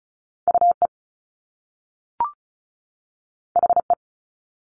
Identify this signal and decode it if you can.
Morse code, heard as a keyed tone, about 36 wpm, 700 Hz. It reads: UE HE
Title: Top Two Letter Combinations in Words 35wpm